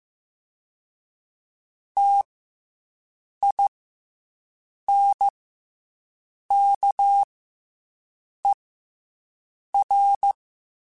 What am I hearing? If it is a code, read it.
Morse code, heard as a keyed tone, about 15 wpm, 800 Hz. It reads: T I N K E R
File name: Special INGs / Special INGs-mp3s / SlowToFaster ing @13